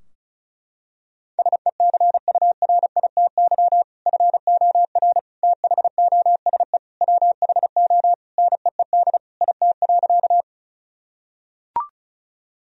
Morse code, heard as a keyed tone, about 34 wpm, 700 Hz. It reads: SECURITY FOR THOSE WHO NEED IT.
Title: Sentences from Top 1000 Words - Review Part 4 - 35wpm